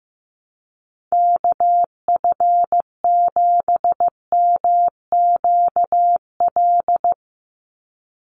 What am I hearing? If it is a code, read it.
Morse code, heard as a keyed tone, about 15 wpm, 700 Hz. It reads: KF7MQL
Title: USA Call Signs 15wpm